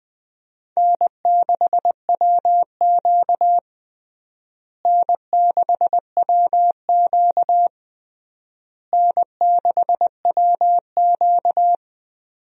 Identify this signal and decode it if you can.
Morse code, heard as a keyed tone, about 21 wpm, 700 Hz. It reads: N6WQ N6WQ N6WQ
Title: USA Call Signs - 4 Characters - Repeated 3x - 20wpm